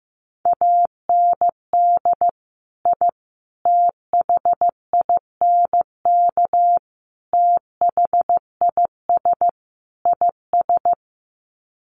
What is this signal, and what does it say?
Morse code, heard as a keyed tone, about 15 wpm, 700 Hz. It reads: AND I THINK THIS IS
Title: Sets of 5 Words Encore 15wpm